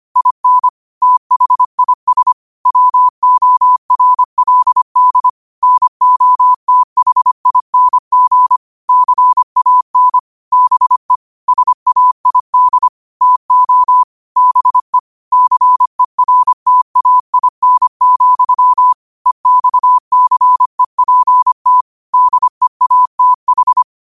Morse code, heard as a keyed tone, about 25 wpm, 1000 Hz. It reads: IN THIS WORLD NOTHING CAN BE SAID TO BE CERTAIN, EXCEPT DEATH
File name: qotdfiles / 25wpm